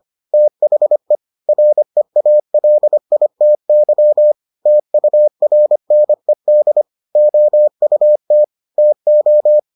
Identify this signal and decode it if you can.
Morse code, heard as a keyed tone, about 25 wpm, 600 Hz. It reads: THE REALITY TURNED OUT TO